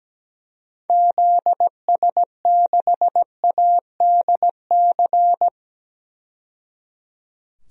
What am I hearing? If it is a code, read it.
Morse code, heard as a keyed tone, about 17 wpm, 700 Hz. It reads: ZS6ADC
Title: DXCC-100 Call Signs - 6 Characters - 17wpm